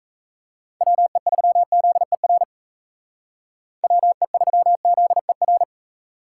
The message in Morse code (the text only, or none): WE3ZER WE3ZER